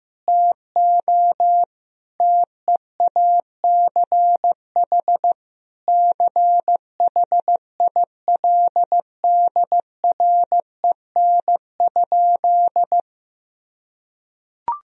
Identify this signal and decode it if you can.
Morse code, heard as a keyed tone, about 16 wpm, 700 Hz. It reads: TO TEACH CHILDREN?
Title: Sentences from Top 1000 Words - Review Part 2 - 15wpm